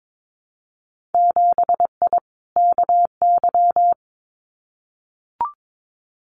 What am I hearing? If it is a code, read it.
Morse code, heard as a keyed tone, about 22 wpm, 700 Hz. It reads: 7I KY
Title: Winter Field Day Exchanges Encore 22wpm